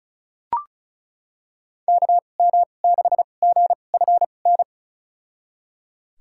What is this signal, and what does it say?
Morse code, heard as a keyed tone, about 36 wpm, spaced as for 25 wpm, 700 Hz. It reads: KM6GFN
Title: USA Call Signs - 6 Characters - 2x Character Spacing - 35wpm